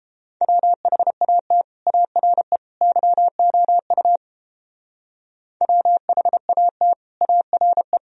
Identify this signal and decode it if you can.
Morse code, heard as a keyed tone, about 29 wpm, 700 Hz. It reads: WHAT ARE YOU WHAT ARE